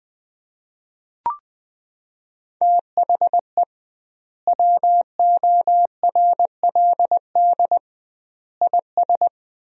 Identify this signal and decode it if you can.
Morse code, heard as a keyed tone, about 21 wpm, 700 Hz. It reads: THE WORLD IS
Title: 2x Word-Spacing - Sentences from Top 400 Words - 20wpm